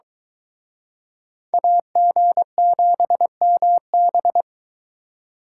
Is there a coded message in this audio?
Yes: Morse code from 1.5 to 4.4 s, heard as a keyed tone, about 23 wpm, 700 Hz. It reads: AG7MB